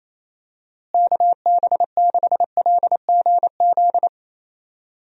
Morse code, heard as a keyed tone, about 28 wpm, 700 Hz. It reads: KB6LGZ
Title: USA Call Signs - 6 Characters - Repeated 2x - 28wpm